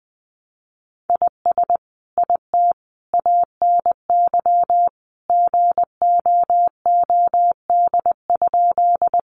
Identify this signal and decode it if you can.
Morse code, heard as a keyed tone, about 20 wpm, 700 Hz. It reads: IS IT ANY GOOD?
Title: Sentences from Top 200 Words 20wpm